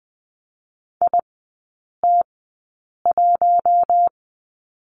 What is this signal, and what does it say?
Morse code, heard as a keyed tone, about 20 wpm, spaced as for 8 wpm, 700 Hz. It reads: IT1
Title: 041 Lesson 8e - Number 1 - Three Characters 20wpm